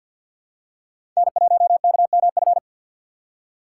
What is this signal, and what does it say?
Morse code, heard as a keyed tone, about 46 wpm, 700 Hz. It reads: N1KMF
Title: USA Call Signs - 5 Characters - 4x Character Spacing - 50wpm